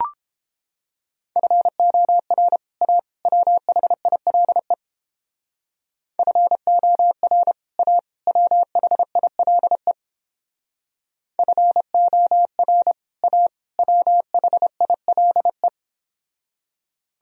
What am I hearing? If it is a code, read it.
Morse code, heard as a keyed tone, about 31 wpm, 700 Hz. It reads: FOR A WHILE FOR A WHILE FOR A WHILE